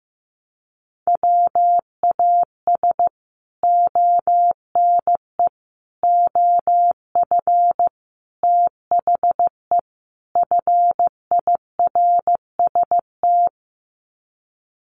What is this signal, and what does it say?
Morse code, heard as a keyed tone, about 15 wpm, 700 Hz. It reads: WAS ONE OF THE FIRST
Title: Sets of 5 Words 15wpm